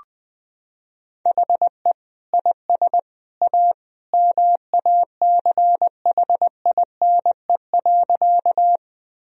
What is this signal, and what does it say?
Morse code, heard as a keyed tone, about 21 wpm, 700 Hz. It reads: HE IS A MACHINE.
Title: Sentences from Top 1000 Words - Review Part 2 - 20wpm